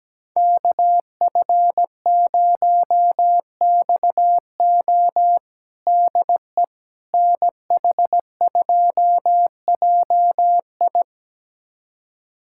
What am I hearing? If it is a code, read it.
Morse code, heard as a keyed tone, about 17 wpm, 700 Hz. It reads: KF0XO DE NH2JI
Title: USA Callsign DE Callsign - 5 Characters - 17wpm